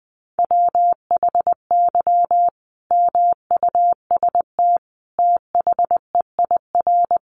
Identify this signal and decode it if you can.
Morse code, heard as a keyed tone, about 20 wpm, 700 Hz. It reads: WHY MUST THEIR
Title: Sentences from Top 1000 Words 20wpm